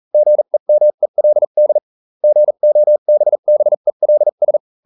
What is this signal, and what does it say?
Morse code, heard as a keyed tone, about 40 wpm, spaced as for 35 wpm, 600 Hz. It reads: GEMELD GOBBELS